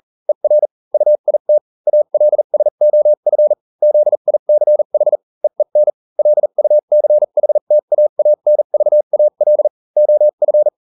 Morse code, heard as a keyed tone, about 40 wpm, spaced as for 35 wpm, 600 Hz. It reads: ER UIT ALSOF ZICH EEN LUCHTAANVAL OF